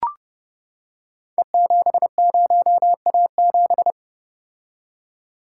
Morse code, heard as a keyed tone, about 28 wpm, 700 Hz. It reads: E70A7
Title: Code Groups - 5 Letter:Numbers - 30wpm